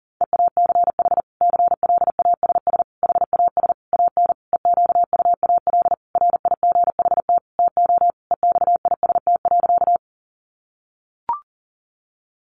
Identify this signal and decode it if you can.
Morse code, heard as a keyed tone, about 40 wpm, 700 Hz. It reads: EACH CLASS HAS AN EQUAL RIGHT TO EXIST.
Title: Sentences from Top 1400 Words 40wpm